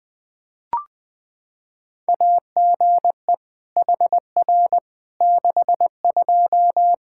Morse code, heard as a keyed tone, about 21 wpm, 700 Hz. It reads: AGE HR 62
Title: Practice QSO - Exchange 3 - More Info - 20wpm